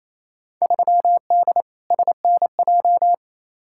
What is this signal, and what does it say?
Morse code, heard as a keyed tone, about 28 wpm, 700 Hz. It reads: 3D SNJ
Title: Field Day Exchanges Encore - 28wpm